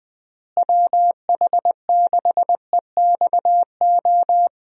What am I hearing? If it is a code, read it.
Morse code, heard as a keyed tone, about 20 wpm, 700 Hz. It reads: WH6EXO
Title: USA Call Signs - 6 Characters - Repeated 3x - 20wpm